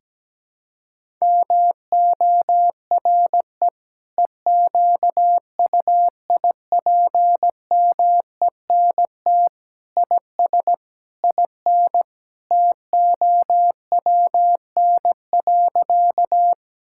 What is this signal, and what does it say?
Morse code, heard as a keyed tone, about 17 wpm, 700 Hz. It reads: MORE EQUIPMENT IS IN TOWN.
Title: Sentences from Top 1000 Words - Review Part 5 - 17wpm